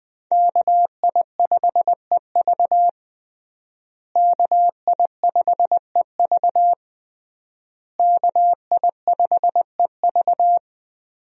Morse code, heard as a keyed tone, about 20 wpm, 700 Hz. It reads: KI5EV KI5EV KI5EV
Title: USA Call Signs - 5 Characters - Repeated 3x - 20wpm